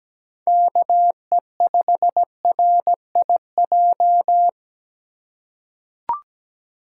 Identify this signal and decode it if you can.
Morse code, heard as a keyed tone, about 17 wpm, 700 Hz. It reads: KE5RIJ
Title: USA Call Signs - 6 Characters - 3x Character Spacing - 17wpm